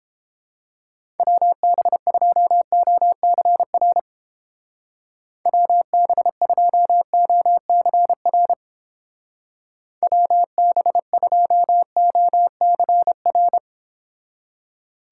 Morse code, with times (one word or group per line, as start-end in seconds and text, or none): 1.2-4.0 WB2OCR
5.5-8.5 WB2OCR
10.0-13.6 WB2OCR